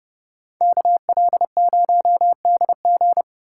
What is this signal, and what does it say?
Morse code, heard as a keyed tone, about 34 wpm, 700 Hz. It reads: KL0DG
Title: Speed Racing - USA 2x2 Call Signs 30wpm